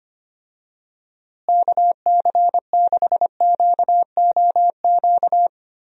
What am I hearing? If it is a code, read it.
Morse code, heard as a keyed tone, about 25 wpm, 700 Hz. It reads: KC6QOQ
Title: USA Call Signs - 6 Characters - Repeated 3x - 25wpm